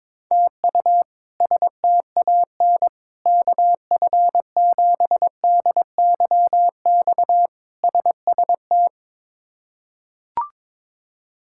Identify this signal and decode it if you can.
Morse code, heard as a keyed tone, about 22 wpm, 700 Hz. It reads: TU STAN KF7DYX SST
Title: K1USN SST QSO Practice - 22wpm